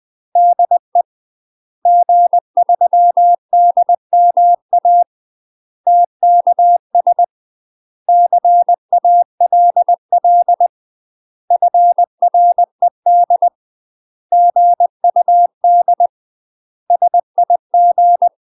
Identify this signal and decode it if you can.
Morse code, heard as a keyed tone, about 20 wpm, 700 Hz. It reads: DE G3DMA TKS CALL FRED GUD SIG
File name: 20WPM